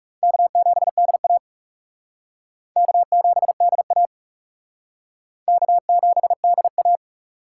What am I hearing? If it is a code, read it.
Morse code, heard as a keyed tone, about 39 wpm, 700 Hz. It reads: K7DA K7DA K7DA